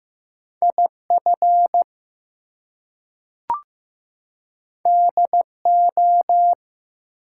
Morse code, heard as a keyed tone, about 15 wpm, 700 Hz. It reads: IF DO
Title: Two Letter Words - 15wpm